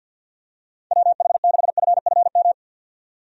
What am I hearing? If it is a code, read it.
Morse code, heard as a keyed tone, about 52 wpm, 700 Hz. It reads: WH6FFM
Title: USA Call Signs - 6 Characters - 50wpm